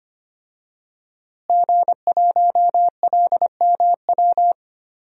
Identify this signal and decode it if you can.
Morse code, heard as a keyed tone, about 25 wpm, 700 Hz. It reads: G1LMW
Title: DXCC-100 Call Signs - 5 Characters - Repeated 2x - 25wpm